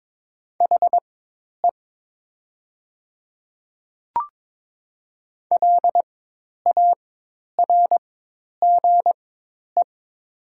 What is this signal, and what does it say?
Morse code, heard as a keyed tone, about 22 wpm, spaced as for 10 wpm, 700 Hz. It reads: HE LARGE
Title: 4x Character Spacing - Top 300 Words - 22wpm